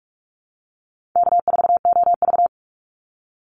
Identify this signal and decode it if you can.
Morse code, heard as a keyed tone, about 44 wpm, 700 Hz. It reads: K4OV
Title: Easy Call Signs Repeated 3x - 45wpm